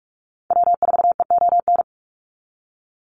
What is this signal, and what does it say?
Morse code, heard as a keyed tone, about 46 wpm, 700 Hz. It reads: W4EON
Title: NA SOTA Activator Callsigns 45wpm